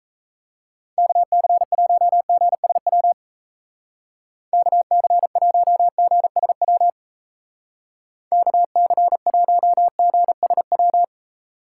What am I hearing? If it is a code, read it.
Morse code, heard as a keyed tone, about 34 wpm, 700 Hz. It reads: KC1GSW KC1GSW KC1GSW
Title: Speed Racing - USA Call Signs 28wpm